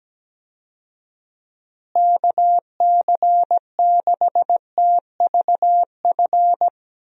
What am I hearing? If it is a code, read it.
Morse code, heard as a keyed tone, about 17 wpm, 700 Hz. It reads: KC6TVF
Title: USA Call Signs - 6 Characters - Repeated 2x - 17wpm